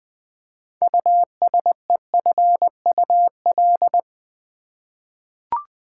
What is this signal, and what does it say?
Morse code, heard as a keyed tone, about 21 wpm, 700 Hz. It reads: USEFUL
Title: Six Letter Words - 20wpm